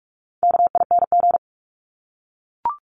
Morse code, heard as a keyed tone, about 44 wpm, 700 Hz. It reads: KING
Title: Top 500 Words Repeated 45wpm